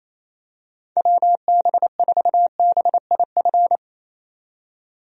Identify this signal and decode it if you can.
Morse code, heard as a keyed tone, about 28 wpm, 700 Hz. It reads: WB4BIF
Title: USA Call Signs - 6 Characters - 28wpm